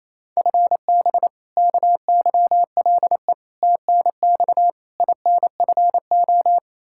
Morse code, heard as a keyed tone, about 28 wpm, 700 Hz. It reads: FB KYLE TNX INFO
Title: Practice QSO - Exchange 3 - More Info - 28wpm